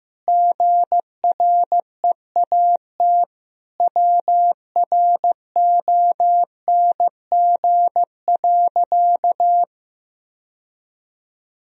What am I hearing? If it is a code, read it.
Morse code, heard as a keyed tone, about 15 wpm, 700 Hz. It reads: GREAT WRONG.
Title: Sentences from Top 1000 Words - Review Part 4 - 15wpm